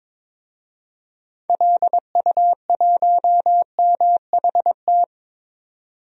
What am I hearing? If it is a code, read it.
Morse code, heard as a keyed tone, about 22 wpm, 700 Hz. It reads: LU1MHT
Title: DXCC-100 Call Signs - 6 Characters - 22wpm